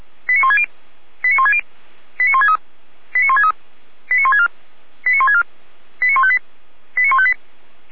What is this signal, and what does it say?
Five-tone selective call (ZVEI), heard as quick runs of five tones, tones of about 70 ms. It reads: 89160 89160 89163 89163 89164 89164 89158 89158